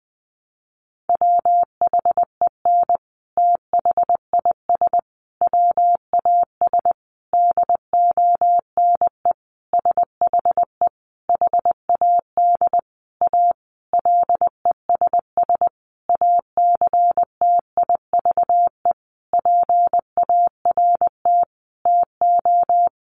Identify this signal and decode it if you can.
Morse code, heard as a keyed tone, about 20 wpm, 700 Hz. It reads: WHEN THIS WAS DONE SHE HAD A LESS ACTIVE PART TO
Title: Sentences from Top 900 Words 20wpm